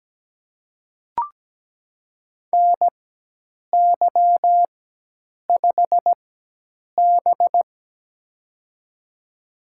Morse code, heard as a keyed tone, about 17 wpm, spaced as for 8 wpm, 700 Hz. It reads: NY5B
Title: USA Call Signs - 4 Characters - 4x Character Spacing - 17wpm